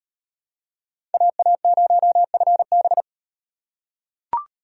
Morse code, heard as a keyed tone, about 36 wpm, 700 Hz. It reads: AA0FB